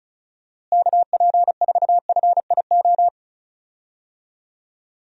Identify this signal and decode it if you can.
Morse code, heard as a keyed tone, about 36 wpm, 700 Hz. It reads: KP4FIO
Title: USA Call Signs - 6 Characters - Repeated 3x - 35wpm